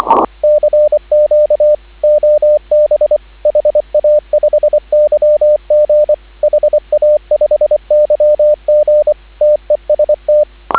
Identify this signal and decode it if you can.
Morse code, heard as a keyed tone, about 25 wpm, 590 Hz, with radio noise. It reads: CQ OB HA5YG HA5YG TEST